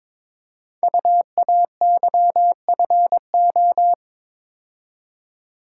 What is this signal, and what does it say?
Morse code, heard as a keyed tone, about 23 wpm, 700 Hz. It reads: UAYFO
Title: Code Groups - 5 Letters - 22wpm